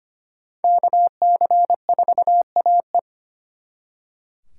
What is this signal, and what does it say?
Morse code, heard as a keyed tone, about 32 wpm, 700 Hz. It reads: KC4AE